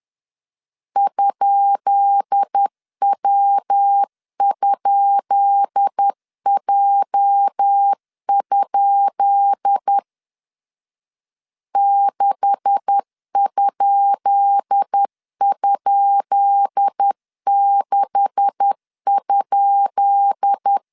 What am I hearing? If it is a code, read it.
Morse code, heard as a keyed tone, about 10 wpm, 800 Hz. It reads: ?W?J? 6??6?
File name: LEZIONI / 10 SIGNS / 10B INT 10wpm 2844